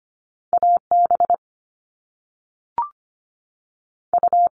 Morse code, heard as a keyed tone, about 25 wpm, 700 Hz. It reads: A B U
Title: All Cut Numbers - 2-Digit 25wpm